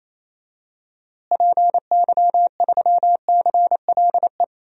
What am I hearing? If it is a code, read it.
Morse code, heard as a keyed tone, about 28 wpm, 700 Hz. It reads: PY3CLE
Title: DXCC-100 Call Signs - 6 Characters - Repeated 2x - 28wpm